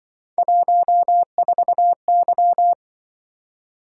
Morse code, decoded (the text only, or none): 14Y